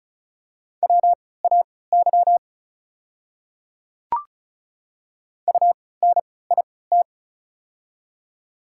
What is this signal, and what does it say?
Morse code, heard as a keyed tone, about 34 wpm, spaced as for 20 wpm, 690 Hz. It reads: WAY UNIT
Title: 3x Character Spacing - Top 500 Words - 35wpm